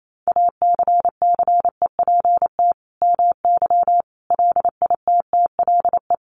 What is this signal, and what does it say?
Morse code, heard as a keyed tone, about 28 wpm, 700 Hz. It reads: ACCEPT MY LITTLE
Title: Sentences from Top 1300 Words 28wpm